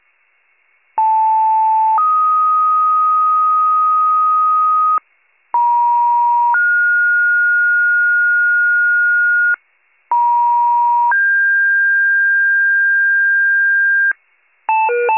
Two-tone page, heard as a one-second tone, then a three-second tone, about 880 then 1250 Hz, about 950 then 1430 Hz, about 950 then 1600 Hz, with noise. It is heard from 1.0 to 14.1 s.